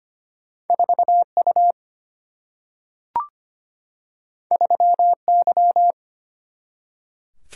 Morse code, heard as a keyed tone, about 25 wpm, 700 Hz. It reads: 4 U 3 Y
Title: Single-Letter Single-Number 25wpm